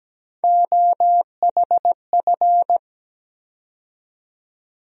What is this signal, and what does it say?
Morse code, heard as a keyed tone, about 17 wpm, 700 Hz. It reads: OHF
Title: USA Call Sign Postfixes - 3 Letters 17wpm